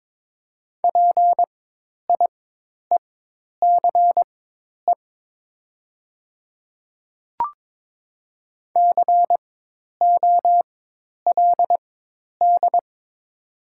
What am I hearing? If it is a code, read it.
Morse code, heard as a keyed tone, about 23 wpm, spaced as for 10 wpm, 700 Hz. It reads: PIECE COLD
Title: 4x Character Spacing - Top 500 Words - 22wpm